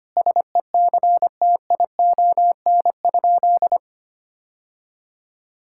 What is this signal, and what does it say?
Morse code, heard as a keyed tone, about 25 wpm, 700 Hz. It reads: SECTION?
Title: Sentences from Top 1000 Words - Review Part 5 - 25wpm